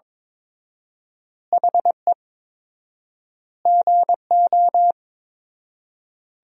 Morse code, heard as a keyed tone, about 22 wpm, 700 Hz. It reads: HE GO
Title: 4x Word-Spacing - Sentences from Top 500 Words - 22wpm